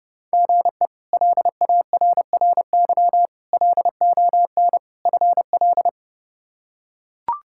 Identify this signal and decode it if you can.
Morse code, heard as a keyed tone, about 30 wpm, 700 Hz. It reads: GE LARRY LON FL
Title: K1USN SST QSO Practice - 30wpm